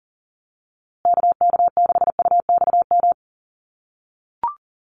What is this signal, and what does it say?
Morse code, heard as a keyed tone, about 40 wpm, 700 Hz. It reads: KK6UXM
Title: USA Call Signs Repeated 2x 40wpm